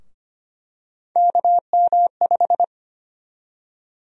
Morse code, heard as a keyed tone, about 25 wpm, 700 Hz. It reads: KM5
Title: USA Call Sign Prefixes - 25wpm